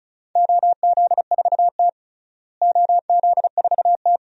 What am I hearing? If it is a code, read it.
Morse code, heard as a keyed tone, about 34 wpm, 700 Hz. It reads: OZ4T OZ4T
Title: DXCC-100 Call Signs - 4 Characters - Repeated 2x - 35wpm